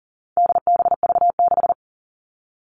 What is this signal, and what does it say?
Morse code, heard as a keyed tone, about 40 wpm, 700 Hz. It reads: DBV6
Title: All Cut Numbers - 4-Digit 40wpm